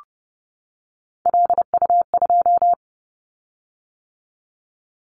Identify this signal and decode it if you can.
Morse code, heard as a keyed tone, about 31 wpm, 700 Hz. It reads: LU2
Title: Callsign Prefixes for Top 50 DXCC - 30wpm